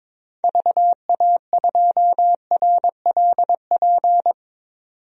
Morse code, heard as a keyed tone, about 21 wpm, 700 Hz. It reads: VA2RLP
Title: DXCC-100 Call Signs - 6 Characters - 22wpm